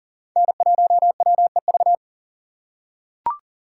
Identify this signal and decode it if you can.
Morse code, heard as a keyed tone, about 40 wpm, 700 Hz. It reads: N1WEV
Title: USA Call Signs - 5 Characters - 4x Character Spacing - 40wpm